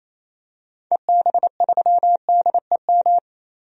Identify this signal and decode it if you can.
Morse code, heard as a keyed tone, about 28 wpm, 700 Hz. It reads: EB3DEM
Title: DXCC-100 Call Signs - 6 Characters - 28wpm